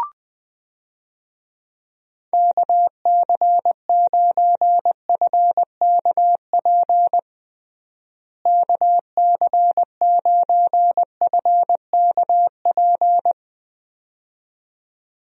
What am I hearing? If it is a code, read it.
Morse code, heard as a keyed tone, about 20 wpm, 700 Hz. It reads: KC9FKP KC9FKP
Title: USA Call Signs - 6 Characters - Repeated 2x - 20wpm